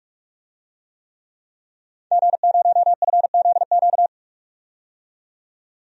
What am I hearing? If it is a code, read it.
Morse code, heard as a keyed tone, about 46 wpm, 700 Hz. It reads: G0FZQ
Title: DXCC-100 Call Signs - 5 Characters - 45wpm